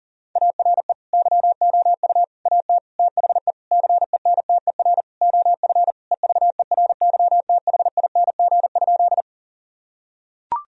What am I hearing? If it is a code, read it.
Morse code, heard as a keyed tone, about 40 wpm, 700 Hz. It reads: ARE YOU AT THE CENTER OF EVERYTHING?
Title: Sentences from Top 1000 Words - Review Part 2 - 40wpm